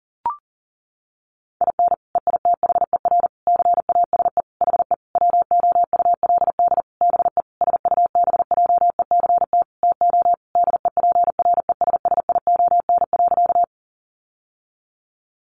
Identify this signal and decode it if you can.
Morse code, heard as a keyed tone, about 40 wpm, 700 Hz. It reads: IN EITHER CASE HE WOULD BE SUBJECT TO DEPRESSION.
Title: Sentences from Top 1300 Words 40wpm